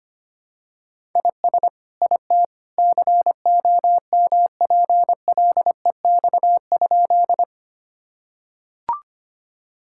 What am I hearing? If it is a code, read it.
Morse code, heard as a keyed tone, about 25 wpm, 700 Hz. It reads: IS IT COMPLEX?
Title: Sentences from Top 1000 Words - Review Part 6 - 25wpm